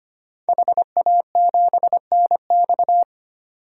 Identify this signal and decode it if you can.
Morse code, heard as a keyed tone, about 25 wpm, 700 Hz. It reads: HA7NX
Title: DXCC-100 Call Signs - 5 Characters - 25wpm